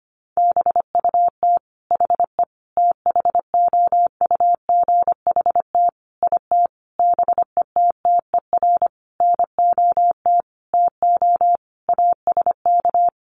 Morse code, heard as a keyed tone, about 25 wpm, 700 Hz. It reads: BUT HE THOUGHT IT BETTER NOT TO ASK
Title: Sentences from Top 600 Words 25wpm